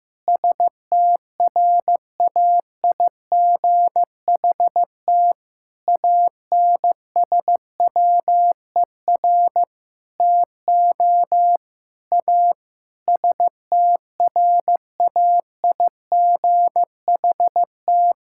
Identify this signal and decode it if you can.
Morse code, heard as a keyed tone, about 15 wpm, 700 Hz. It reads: STRAIGHT ANSWER TO A STRAIGHT
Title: Sentences from Top 1000 Words - Review Part 6 - 15wpm